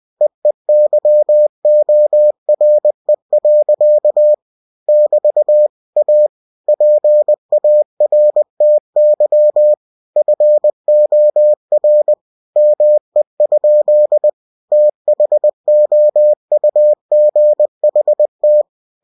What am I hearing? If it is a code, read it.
Morse code, heard as a keyed tone, about 20 wpm, 600 Hz. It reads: EEYORE. <BT> A PARTY FOR ME? THOUGHT
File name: cw-books / winnie-the-pooh / winnie-20wpm-page-0103